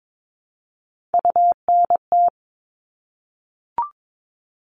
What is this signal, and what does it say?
Morse code, heard as a keyed tone, about 22 wpm, 700 Hz. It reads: UNT
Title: Top Three Letter Combinations in Words 22wpm